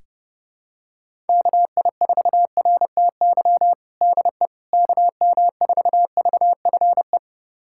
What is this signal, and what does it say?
Morse code, heard as a keyed tone, about 30 wpm, 700 Hz. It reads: KI4RTY DE KM4VFE
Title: USA Callsign DE Callsign - 6 Characters - 30wpm